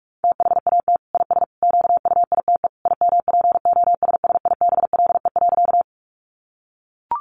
Morse code, heard as a keyed tone, about 44 wpm, 700 Hz. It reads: THAT IS QUITE IMPOSSIBLE.
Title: Sentences from Top 1100 Words 45wpm